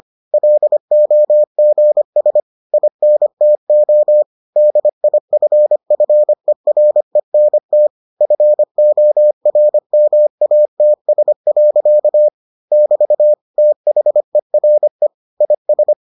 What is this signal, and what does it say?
Morse code, heard as a keyed tone, about 25 wpm, 600 Hz. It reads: LOGS INTO DIFFERENT FORMATS. <BT> THERE IS